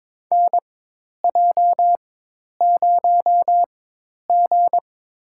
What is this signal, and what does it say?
Morse code, heard as a keyed tone, about 21 wpm, spaced as for 10 wpm, 700 Hz. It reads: NJ0G
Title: USA Call Signs - 4x Character Spacing - 22wpm